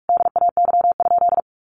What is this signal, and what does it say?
Morse code, heard as a keyed tone, about 44 wpm, 700 Hz. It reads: DAY?
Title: Sentences from Top 1500 Words 45wpm